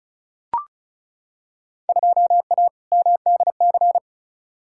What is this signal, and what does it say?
Morse code, heard as a keyed tone, about 36 wpm, 690 Hz. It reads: 2A MDC
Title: Field Day Exchanges Encore - 35wpm